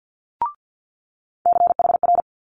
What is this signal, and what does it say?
Morse code, heard as a keyed tone, about 50 wpm, 700 Hz. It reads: CHR
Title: Multi-letter Phonemes Quick Fire 50wpm